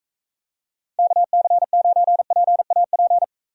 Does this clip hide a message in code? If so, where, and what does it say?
Yes: Morse code from 1.0 to 3.2 s, heard as a keyed tone, about 44 wpm, 700 Hz. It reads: KC9PAP